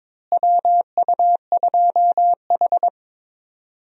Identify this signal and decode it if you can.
Morse code, heard as a keyed tone, about 22 wpm, 700 Hz. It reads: WU2H
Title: USA Call Signs - 4 Characters - Repeated 3x - 22wpm